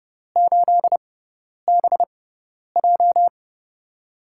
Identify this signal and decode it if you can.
Morse code, heard as a keyed tone, about 32 wpm, spaced as for 10 wpm, 700 Hz. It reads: 8BJ
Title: USA Call Signs - 6x Character Spacing - 30wpm